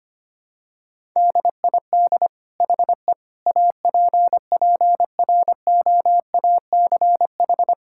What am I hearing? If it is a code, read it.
Morse code, heard as a keyed tone, about 25 wpm, 700 Hz. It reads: DID HE APPROACH
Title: Sentences from Top 1000 Words - Review Part 6 - 25wpm